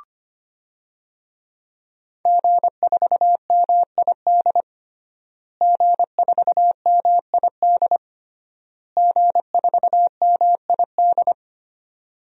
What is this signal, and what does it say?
Morse code, heard as a keyed tone, about 25 wpm, 700 Hz. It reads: G4MID G4MID G4MID
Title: DXCC-100 Call Signs - 5 Characters - Repeated 3x - 25wpm